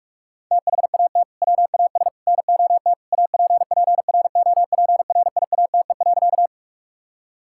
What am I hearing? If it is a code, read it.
Morse code, heard as a keyed tone, about 41 wpm, 700 Hz. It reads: THAT WAS NOT APPROPRIATE.
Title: Sentences from Top 1000 Words - Review Part 5 - 45wpm